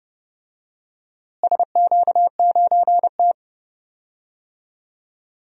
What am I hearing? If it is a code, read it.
Morse code, heard as a keyed tone, about 32 wpm, 700 Hz. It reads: SQ9T
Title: DXCC-100 Call Signs - 4 Characters - Repeated 2x - 30wpm